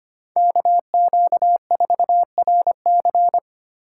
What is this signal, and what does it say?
Morse code, heard as a keyed tone, about 25 wpm, 700 Hz. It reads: KQ4RC
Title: USA Call Signs - 5 Characters - 3x Character Spacing - 25wpm